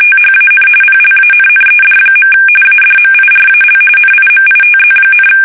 RTTY, about 45.45 baud. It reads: THAN PLANNED. ADMIRAL NIMITZ,
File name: fsk850 / battleshipnj